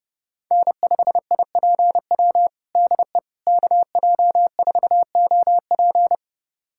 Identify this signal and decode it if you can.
Morse code, heard as a keyed tone, about 32 wpm, 700 Hz. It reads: N5IPW DE KJ4OP
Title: USA Callsign DE Callsign - 5 Characters - 30wpm